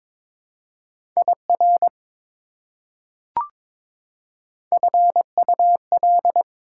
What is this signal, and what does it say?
Morse code, heard as a keyed tone, about 22 wpm, 700 Hz. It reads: IR FUL
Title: Prefixes and Suffixes 22wpm